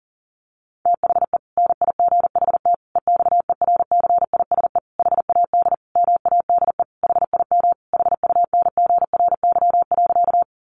Morse code, heard as a keyed tone, about 40 wpm, 700 Hz. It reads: THE NIGHT EXERCISE HAD MADE HIM HUNGRY.
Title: Sentences from Top 1400 Words 40wpm